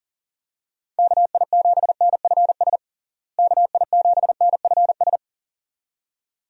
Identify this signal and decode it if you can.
Morse code, heard as a keyed tone, about 40 wpm, 700 Hz. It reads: KI7NFS KI7NFS
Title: USA Call Signs - 6 Characters - Repeated 2x - 40wpm